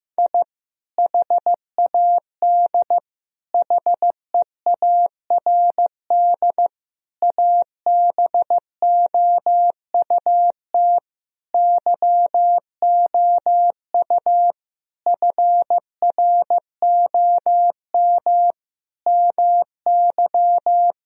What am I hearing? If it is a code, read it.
Morse code, heard as a keyed tone, about 15 wpm, 700 Hz. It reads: I HAD HEARD ABOUT YOU FROM MY
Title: Sentences from Top 1000 Words - Review Part 1 - 15wpm